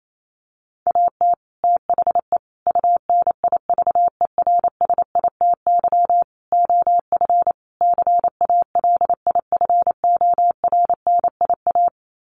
Morse code, heard as a keyed tone, about 28 wpm, 700 Hz. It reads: AT THE UNIVERSITY OF CALIFORNIA
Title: Sets of 5 Words Encore2 28wpm